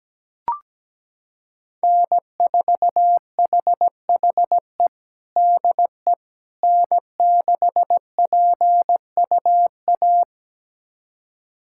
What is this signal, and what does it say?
Morse code, heard as a keyed tone, about 17 wpm, 700 Hz. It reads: N4HHE DE N6PUA
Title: USA Callsign DE Callsign - 5 Characters - 17wpm